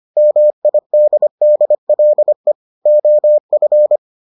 Morse code, heard as a keyed tone, about 25 wpm, 600 Hz. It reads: MIDDLE OF